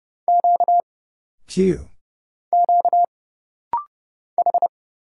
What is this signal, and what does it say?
Morse code, heard as a keyed tone, about 30 wpm, 700 Hz. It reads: Q Q H
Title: Single Letter-Number Alt - Rapid-Fire 30wpm